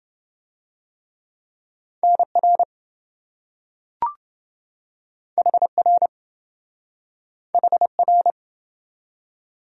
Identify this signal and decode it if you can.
Morse code, heard as a keyed tone, about 28 wpm, 700 Hz. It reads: NR HR HR